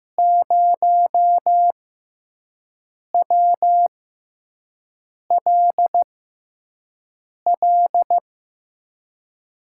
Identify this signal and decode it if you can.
Morse code, heard as a keyed tone, about 15 wpm, spaced as for 5 wpm, 700 Hz. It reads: 0WLL
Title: USA Call Signs - 5 Characters - 6x Character Spacing - 15wpm